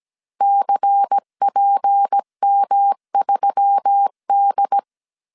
Morse code, heard as a keyed tone, about 17 wpm, 800 Hz. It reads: CPM3D
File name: CW / DET / 1 9 1500C S C 16wpm 2909